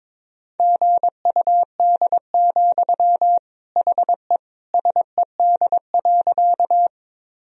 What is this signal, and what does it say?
Morse code, heard as a keyed tone, about 22 wpm, 700 Hz. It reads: GUD, HE SED.
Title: Sentences from Top 200 Words and Abbreviations 22wpm